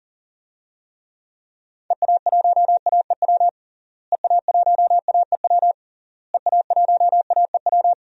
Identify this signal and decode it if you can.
Morse code, heard as a keyed tone, about 37 wpm, 700 Hz. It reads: EA1AEW EA1AEW EA1AEW
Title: DXCC-100 Call Signs - 6 Characters - Repeated 3x - 40wpm